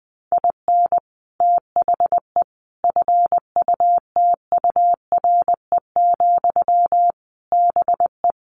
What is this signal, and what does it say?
Morse code, heard as a keyed tone, about 20 wpm, 700 Hz. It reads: IN THE FUTURE, BE
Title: Sentences from Top 1300 Words 20wpm